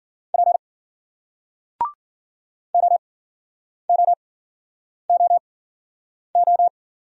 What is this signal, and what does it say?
Morse code, heard as a keyed tone, about 51 wpm, spaced as for 9 wpm, 700 Hz. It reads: P OOOO